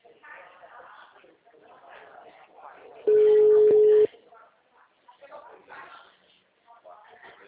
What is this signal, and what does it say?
Ringback tone (Europe, 425 Hz), with speech.